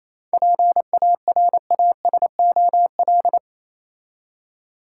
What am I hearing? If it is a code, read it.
Morse code, heard as a keyed tone, about 27 wpm, 700 Hz. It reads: PARASOL
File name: Easy Words - Speed-Racing with Hints - 22wpm